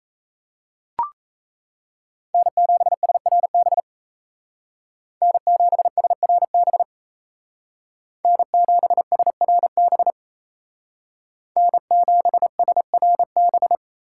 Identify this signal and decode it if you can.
Morse code, heard as a keyed tone, about 33 wpm, 700 Hz. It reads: N7SRB N7SRB N7SRB N7SRB
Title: Speed Racing - USA 1x3 Call Signs 28wpm